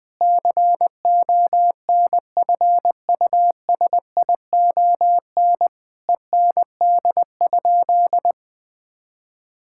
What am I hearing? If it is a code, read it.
Morse code, heard as a keyed tone, about 20 wpm, 700 Hz. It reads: CONFUSION END?
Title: Sentences from Top 1600 Words 20wpm